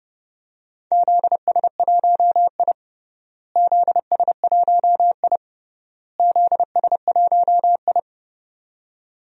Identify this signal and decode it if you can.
Morse code, heard as a keyed tone, about 30 wpm, 700 Hz. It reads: ZS1I ZS1I ZS1I
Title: DXCC-100 Call Signs - 4 Characters - Repeated 3x - 30wpm